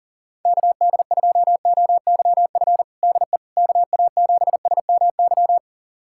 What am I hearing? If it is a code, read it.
Morse code, heard as a keyed tone, about 40 wpm, 690 Hz. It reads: KD2OYF DE KA7SMY
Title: USA Callsign DE Callsign - 6 Characters - 40wpm